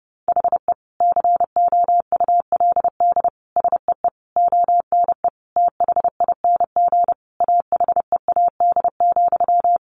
Morse code, heard as a keyed tone, about 28 wpm, 700 Hz. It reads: HE COULD SEE ONE THING AHEAD,
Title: Sentences from Top 1100 Words 30wpm